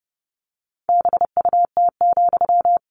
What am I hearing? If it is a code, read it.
Morse code, heard as a keyed tone, about 28 wpm, 700 Hz. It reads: BUT,
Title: Sentences from Top 600 Words 30wpm